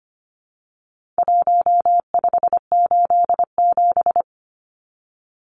Morse code, heard as a keyed tone, about 25 wpm, 700 Hz. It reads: 1587
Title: Numbers - 4-Digit 25wpm